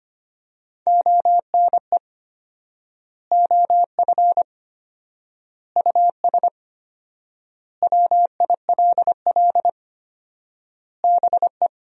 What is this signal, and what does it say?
Morse code, heard as a keyed tone, about 25 wpm, 700 Hz. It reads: ONE OF US WILL BE
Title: 4x Word-Spacing - Sentences from Top 300 Words - 25wpm